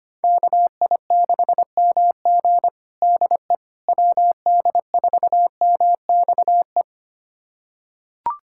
Morse code, heard as a keyed tone, about 25 wpm, 700 Hz. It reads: KI6MG DE WD4MXE
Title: USA Callsign DE Callsign 25wpm